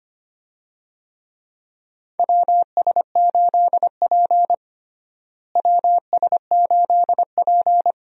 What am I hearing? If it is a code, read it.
Morse code, heard as a keyed tone, about 25 wpm, 700 Hz. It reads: WS8P WS8P
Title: USA Call Signs - 4 Characters - Repeated 2x - 25wpm